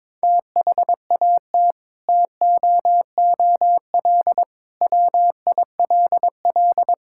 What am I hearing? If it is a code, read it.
Morse code, heard as a keyed tone, about 22 wpm, 700 Hz. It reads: THAT TOOL WILL
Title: Sentences from Top 1000 Words - Review Part 6 - 22wpm